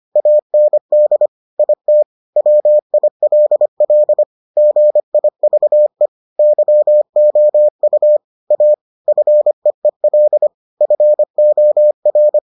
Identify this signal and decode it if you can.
Morse code, heard as a keyed tone, about 25 wpm, 600 Hz. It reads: AND IT WILL GIVE YOU A FEEL FOR